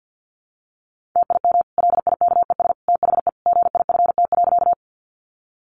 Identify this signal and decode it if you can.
Morse code, heard as a keyed tone, about 50 wpm, 700 Hz. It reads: TIM LIKES THE GIFT.
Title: Easy Phrases 50wpm